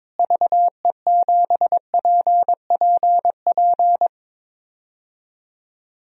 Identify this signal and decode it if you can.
Morse code, heard as a keyed tone, about 22 wpm, 700 Hz. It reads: VE7PPP
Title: DXCC-100 Call Signs - 6 Characters - Repeated 3x - 22wpm